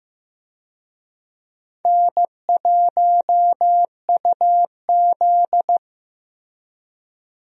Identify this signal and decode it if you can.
Morse code, heard as a keyed tone, about 15 wpm, 700 Hz. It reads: N1UZ
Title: USA Call Signs - 4 Characters - Repeated 2x - 15wpm